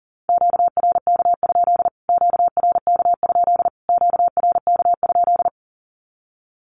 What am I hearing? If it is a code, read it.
Morse code, heard as a keyed tone, about 38 wpm, 700 Hz. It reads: QRK? QRK? QRK?
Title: Q-Codes Repeated 3x 40wpm